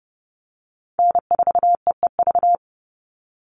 Morse code, heard as a keyed tone, about 32 wpm, 700 Hz. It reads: N4EEV
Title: Call Signs - 30wpm